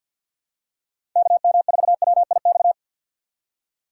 Morse code, heard as a keyed tone, about 50 wpm, 700 Hz. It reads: KM4WIX
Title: USA Call Signs - 2x Character Spacing - 50wpm